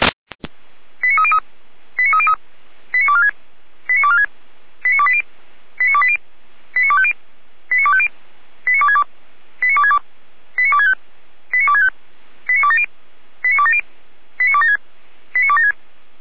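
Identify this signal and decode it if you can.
Five-tone selective call (ZVEI), heard as quick runs of five tones, tones of about 70 ms. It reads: 89303 89303 89246 89246 89280 89280 89250 89250 89262 89262 89265 89265 89270 89270 89276 89276
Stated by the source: Five-tone ZVEI selective call 89167, received on an SDR, from the Feuerwehr Orlamünde fire brigade audio server.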